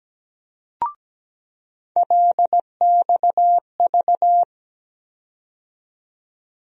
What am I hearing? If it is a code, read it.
Morse code, heard as a keyed tone, about 17 wpm, 700 Hz. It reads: LXV
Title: USA Call Sign Postfixes - 3 Letters 17wpm